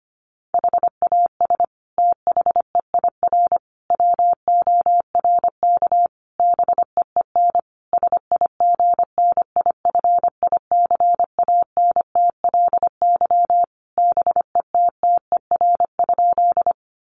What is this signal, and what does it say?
Morse code, heard as a keyed tone, about 25 wpm, 700 Hz. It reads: HAS THEIR WORK BEEN SIGNIFICANTLY BETTER?
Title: Sentences from Top 1500 Words 25wpm